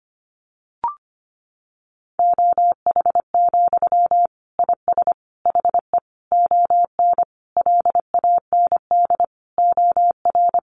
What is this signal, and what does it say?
Morse code, heard as a keyed tone, about 25 wpm, 700 Hz. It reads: OH, IS HE ON LAND OR
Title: Sentences from Top 500 Words 25wpm